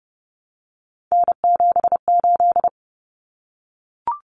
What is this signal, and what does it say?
Morse code, heard as a keyed tone, about 30 wpm, 700 Hz. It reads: N78
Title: Common Cut Numbers - 3-Digit 30wpm